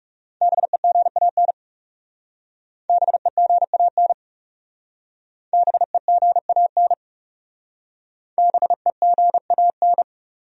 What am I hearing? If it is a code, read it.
Morse code, heard as a keyed tone, about 35 wpm, 700 Hz. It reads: BEGAN BEGAN BEGAN BEGAN
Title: Speed-Racing Top 300 Words 30wpm